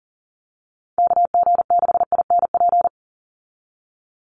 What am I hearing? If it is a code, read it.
Morse code, heard as a keyed tone, about 40 wpm, 700 Hz. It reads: KG6INP
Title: USA Call Signs 40wpm